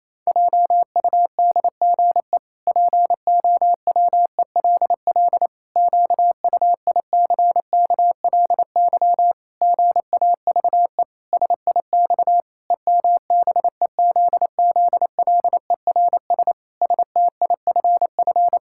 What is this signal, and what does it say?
Morse code, heard as a keyed tone, about 28 wpm, 700 Hz. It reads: JUDGE POWELL QUICKLY GAVE SIX EMBEZZLERS STIFF
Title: Pangram Challenge v2 - 28wpm